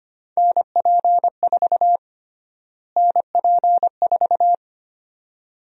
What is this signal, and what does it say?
Morse code, heard as a keyed tone, about 25 wpm, 700 Hz. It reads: NP4 NP4
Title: USA Call Sign Prefixes - Repeated 2x - 25wpm